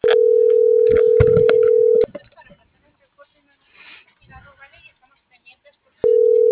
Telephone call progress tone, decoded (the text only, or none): ringback tone